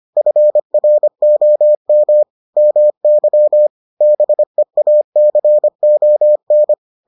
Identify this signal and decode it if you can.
Morse code, heard as a keyed tone, about 25 wpm, 600 Hz. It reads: FROM MY BEACON